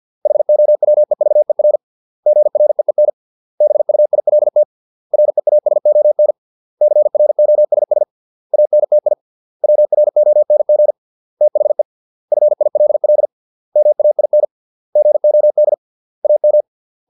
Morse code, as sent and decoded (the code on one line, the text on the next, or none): .... --- .-- . ...- . .-. / --. .-. . -. / -... ..- .. .-.. - / .-. .- ... --- -. / -.-. .-. --- ... ... / .- -. - .. / .-- .-. --- -. --. / - .... / ..-. .. .-.. .-.. / -- .- .. -. / --. --- -.. / .- --
HOWEVER GREEN BUILT REASON CROSS ANTI WRONG THE FILL MAIN GOD AM